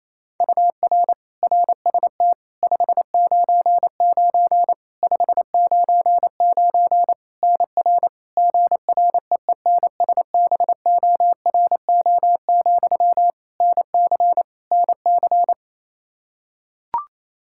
Morse code, heard as a keyed tone, about 28 wpm, 700 Hz. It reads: UR RST 599 599 NR GREENSBORO, NC NC
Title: Practice QSO - Exchange 1 - Beginning - 28wpm